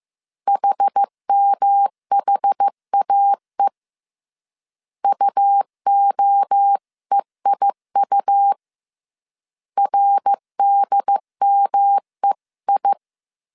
Morse code, heard as a keyed tone, about 15 wpm, 800 Hz. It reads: HMHAE UOEIU RDMEI